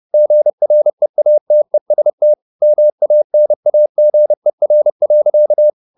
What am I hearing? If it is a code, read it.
Morse code, heard as a keyed tone, about 30 wpm, 600 Hz. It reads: GREATEST MANAGER.
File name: the old man and the sea / the old man and the sea 1 30wpm